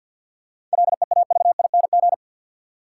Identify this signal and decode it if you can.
Morse code, heard as a keyed tone, about 52 wpm, 700 Hz. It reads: LEAVING
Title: Sentences from Top 1000 Words - Review Part 4 - New Word Drill - 50wpm